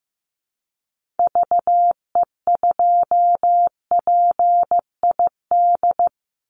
Morse code, heard as a keyed tone, about 15 wpm, 700 Hz. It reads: VE2PID
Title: Call Signs Repeated 3x - 15wpm